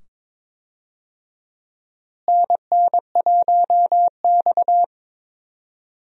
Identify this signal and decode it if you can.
Morse code, heard as a keyed tone, about 22 wpm, 700 Hz. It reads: NN1X
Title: USA Call Signs - 4 Characters - Repeated 2x - 22wpm